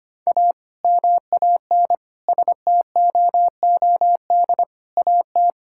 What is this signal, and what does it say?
Morse code, heard as a keyed tone, about 25 wpm, 700 Hz. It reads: A MAN STOOD AT
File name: practice / 25wpm / Sentences with 2-Word Phrases - 25wpm